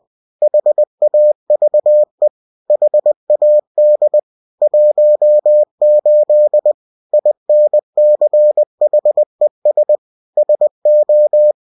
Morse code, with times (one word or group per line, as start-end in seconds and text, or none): 0.4-2.3 HAVE
2.7-4.2 HAD
4.6-6.7 18
7.1-10.0 INCHES
10.4-11.5 SO